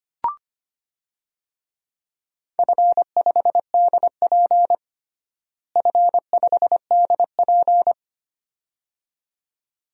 Morse code, heard as a keyed tone, about 25 wpm, 700 Hz. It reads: F5DP F5DP
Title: DXCC-100 Call Signs - 4 Characters - Repeated 2x - 25wpm